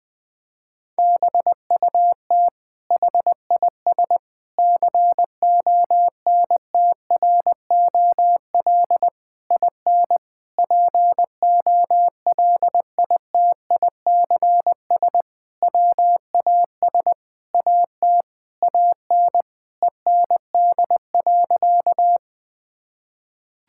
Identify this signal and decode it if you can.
Morse code, heard as a keyed tone, about 20 wpm, 700 Hz. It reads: BUT HIS CONTROL IN POLITICS WAS AT AN END.
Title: Sentences from Top 1000 Words - Review Part 6 - 20wpm